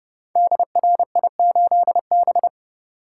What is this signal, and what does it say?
Morse code, heard as a keyed tone, about 30 wpm, 700 Hz. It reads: DRI8B
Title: Code Groups - 5 Letter:Numbers - 30wpm